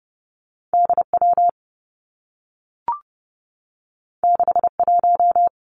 Morse code, heard as a keyed tone, about 30 wpm, 700 Hz. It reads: D W 6 1
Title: Two Challenging Letter-Number Combinations 30wpm